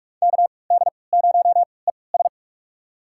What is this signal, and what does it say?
Morse code, heard as a keyed tone, about 46 wpm, spaced as for 26 wpm, 700 Hz. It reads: KD0ES
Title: USA Call Signs - 5 Characters - 3x Character Spacing - 45wpm